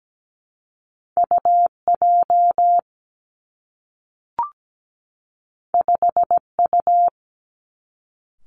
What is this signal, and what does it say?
Morse code, heard as a keyed tone, about 17 wpm, 700 Hz. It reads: UJ 5U
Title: Two Challenging Letter-Number Combinations 17wpm